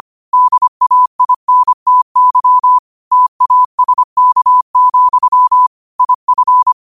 Morse code, heard as a keyed tone, about 25 wpm, 1000 Hz. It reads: DAINTY TASK, IF
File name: qotdfiles / 25wpm